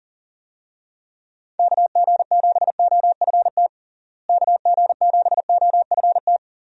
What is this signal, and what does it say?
Morse code, heard as a keyed tone, about 37 wpm, 700 Hz. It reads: KG7OFT KG7OFT
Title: USA Call Signs - 6 Characters - Repeated 3x - 40wpm